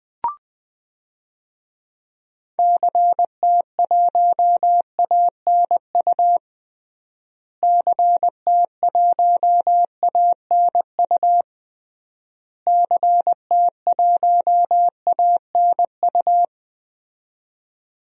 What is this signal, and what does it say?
Morse code, heard as a keyed tone, about 20 wpm, 700 Hz. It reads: CT1ANU CT1ANU CT1ANU
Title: DXCC-100 Call Signs - 6 Characters - Repeated 3x - 20wpm